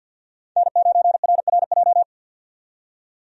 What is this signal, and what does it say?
Morse code, heard as a keyed tone, about 50 wpm, 700 Hz. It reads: N9RRJ
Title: USA Call Signs - 5 Characters - 50wpm